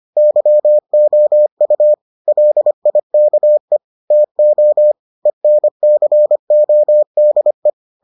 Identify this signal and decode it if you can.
Morse code, heard as a keyed tone, about 25 wpm, 600 Hz. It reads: YOU LIKE TO ENCODE